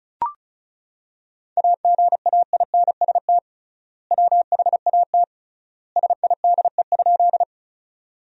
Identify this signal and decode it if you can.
Morse code, heard as a keyed tone, about 36 wpm, 700 Hz. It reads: AGAINST WHAT SIDE?
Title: 3x Word-Spacing - Sentences from Top 400 Words - 35wpm